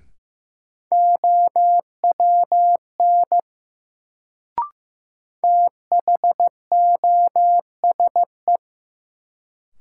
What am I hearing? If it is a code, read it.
Morse code, heard as a keyed tone, about 15 wpm, 700 Hz. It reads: OWN THOSE
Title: Sentences from Top 300 Words - New Word Drill - 15wpm